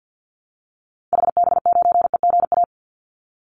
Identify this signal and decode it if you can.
Morse code, heard as a keyed tone, about 50 wpm, 700 Hz. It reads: HB9EGA
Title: Call Signs - 50wpm